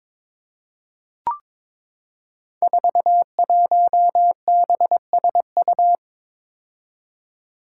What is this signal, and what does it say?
Morse code, heard as a keyed tone, about 23 wpm, 700 Hz. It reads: 41BSU
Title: Five Challenging Letter-Number Combinations 22wpm